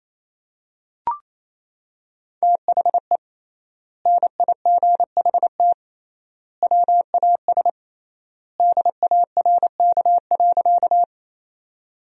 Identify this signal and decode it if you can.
Morse code, heard as a keyed tone, about 28 wpm, 700 Hz. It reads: THE NIGHT WAS DARK.
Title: 3x Word-Spacing - Sentences from Top 500 Words - 28wpm